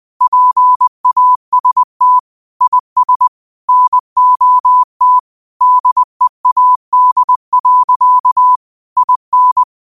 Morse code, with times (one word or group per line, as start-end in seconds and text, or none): 0.2-2.2 PAST
2.6-3.3 IS
3.7-5.2 NOT
5.6-8.6 DEAD.
9.0-9.6 IN